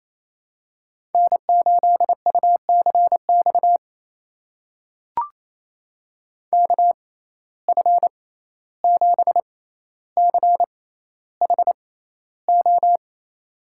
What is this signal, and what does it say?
Morse code, heard as a keyed tone, about 27 wpm, spaced as for 14 wpm, 700 Hz. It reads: N8UCX KF7CHO
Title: USA Call Signs - 6x Character Spacing - 28wpm